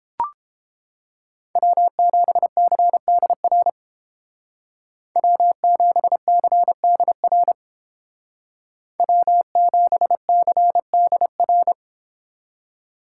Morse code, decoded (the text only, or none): W7CDR W7CDR W7CDR